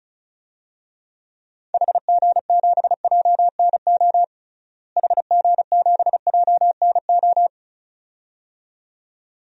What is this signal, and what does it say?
Morse code, heard as a keyed tone, about 36 wpm, 700 Hz. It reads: HG7JNO HG7JNO
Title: DXCC-100 Call Signs - 6 Characters - Repeated 2x - 35wpm